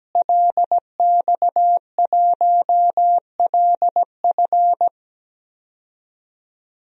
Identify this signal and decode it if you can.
Morse code, heard as a keyed tone, about 17 wpm, 700 Hz. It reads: LX1LF
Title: DXCC-100 Call Signs - 5 Characters - Repeated 3x - 17wpm